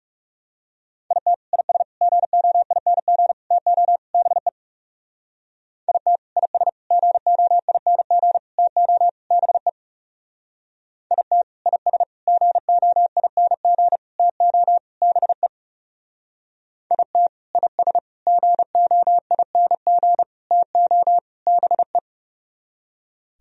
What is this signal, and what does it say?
Morse code, heard as a keyed tone, about 37 wpm, 700 Hz. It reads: IT IS GOING TO BE IT IS GOING TO BE IT IS GOING TO BE IT IS GOING TO BE
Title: Speed-Racing - Sets of 5 Words 30wpm